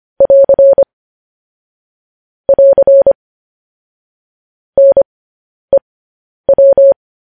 Morse code, heard as a keyed tone, about 25 wpm, spaced as for 10 wpm, 550 Hz. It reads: <AR> <AR> NEW